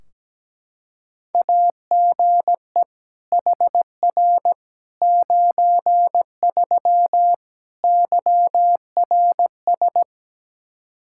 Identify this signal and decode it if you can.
Morse code, heard as a keyed tone, about 17 wpm, 700 Hz. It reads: AGE HR 93 YRS
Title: Practice QSO - Exchange 3 - More Info - 17wpm